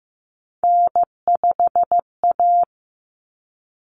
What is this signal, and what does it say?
Morse code, heard as a keyed tone, about 15 wpm, 700 Hz. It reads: N5A
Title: Common Cut Numbers - 3-Digit 15wpm